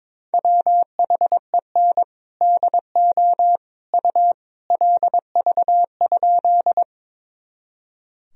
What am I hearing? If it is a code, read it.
Morse code, heard as a keyed tone, about 22 wpm, 700 Hz. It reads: WHEN DO U LV?
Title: Sentences from Top 400 Words and Abbreviations 22wpm